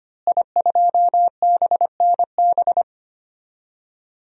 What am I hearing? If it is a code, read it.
Morse code, heard as a keyed tone, about 25 wpm, 700 Hz. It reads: I2BNB
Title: DXCC-100 Call Signs - 5 Characters - Repeated 2x - 25wpm